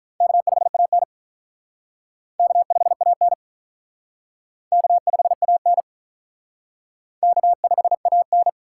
Morse code, heard as a keyed tone, about 47 wpm, 700 Hz. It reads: K5AN K5AN K5AN K5AN